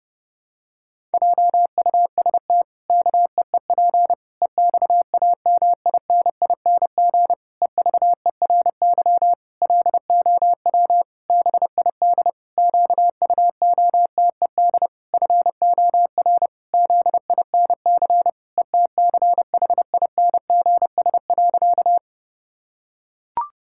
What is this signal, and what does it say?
Morse code, heard as a keyed tone, about 32 wpm, 700 Hz. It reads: JUST KEEP EXAMINING EVERY LOW BID QUOTED FOR ZINC ETCHINGS.
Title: Pangram Challenge v2 - 30wpm